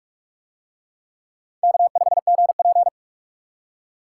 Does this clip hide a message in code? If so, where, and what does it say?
Yes: Morse code from 1.6 to 2.9 s, heard as a keyed tone, about 39 wpm, 700 Hz. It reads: K5GP